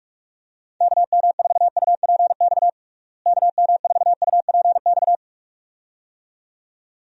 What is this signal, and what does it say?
Morse code, heard as a keyed tone, about 41 wpm, 700 Hz. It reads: KM4UPX KM4UPX
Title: USA Call Signs - 6 Characters - Repeated 2x - 45wpm